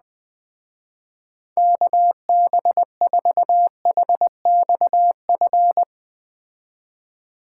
Morse code, heard as a keyed tone, about 20 wpm, 700 Hz. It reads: KB4HXF
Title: Speed Racing - USA 2x3 Call Signs 17wpm